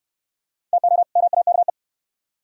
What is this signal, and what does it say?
Morse code, heard as a keyed tone, about 67 wpm, 700 Hz. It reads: TO MAKE